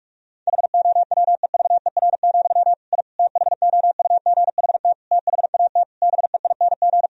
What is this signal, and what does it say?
Morse code, heard as a keyed tone, about 46 wpm, 700 Hz. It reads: HOWEVER, I THOUGHT THAT BEING
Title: Sentences from Top 1000 Words - Review Part 3 - 45wpm